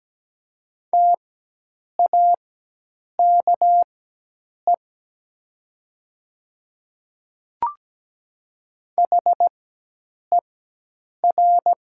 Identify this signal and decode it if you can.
Morse code, heard as a keyed tone, about 17 wpm, spaced as for 8 wpm, 700 Hz. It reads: TAKE HER
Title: 4x Character Spacing - Top 200 Words - 17wpm